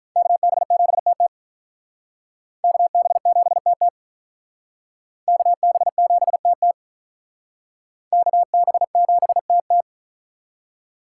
Morse code, heard as a keyed tone, about 45 wpm, 700 Hz. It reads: KB7TT KB7TT KB7TT KB7TT